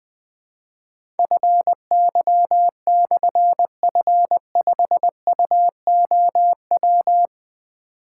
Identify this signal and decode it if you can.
Morse code, heard as a keyed tone, about 20 wpm, 700 Hz. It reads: FY/F5UOW
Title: Difficult Call Signs 20wpm